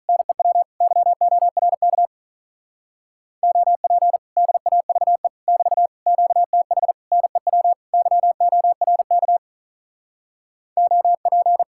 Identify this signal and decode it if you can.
Morse code, heard as a keyed tone, about 43 wpm, 700 Hz. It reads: NEW YORK OP DAVE <BT> QTH NEW YORK OP
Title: Speed-Racing Common QSO Phrases Encore - 35wpm